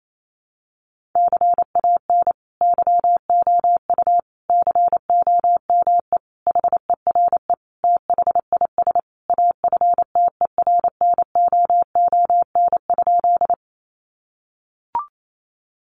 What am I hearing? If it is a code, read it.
Morse code, heard as a keyed tone, about 28 wpm, 700 Hz. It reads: CAN YOU COME HERE THIS AFTERNOON?
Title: Sentences from Top 1500 Words 28wpm